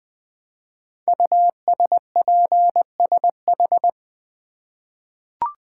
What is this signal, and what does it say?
Morse code, heard as a keyed tone, about 20 wpm, 700 Hz. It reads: USPSH
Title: Code Groups - 5 Letters - 20wpm